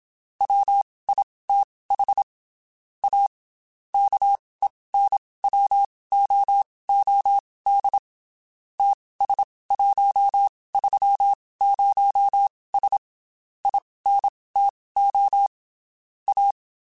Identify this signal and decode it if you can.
Morse code, heard as a keyed tone, about 27 wpm, spaced as for 19 wpm, 800 Hz. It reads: WITH A KENWOOD TS130S INTO A
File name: QSOs / QSOs-mp3s / WD0BGP 1@25